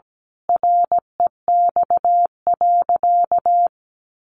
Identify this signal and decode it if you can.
Morse code, heard as a keyed tone, about 17 wpm, 700 Hz. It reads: REX.
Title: A Sound Of Thunder 17wpm